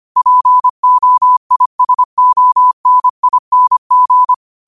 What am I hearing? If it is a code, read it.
Morse code, heard as a keyed tone, about 25 wpm, 1000 Hz. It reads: POISONING